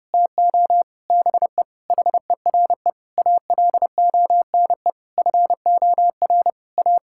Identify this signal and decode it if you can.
Morse code, heard as a keyed tone, about 30 wpm, 700 Hz. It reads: TO BE HERE ALONE FOR A
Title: Sentences from Top 1000 Words - Review Part 5 - 30wpm